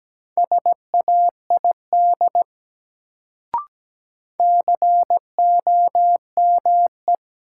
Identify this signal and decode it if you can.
Morse code, heard as a keyed tone, about 17 wpm, 700 Hz. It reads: SAID COME
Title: Sentences from Top 100 Words - New Word Drill - 17wpm